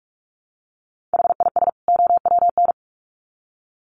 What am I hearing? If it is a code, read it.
Morse code, heard as a keyed tone, about 44 wpm, 700 Hz. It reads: HIS OWN
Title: Sets of 2 Words 45wpm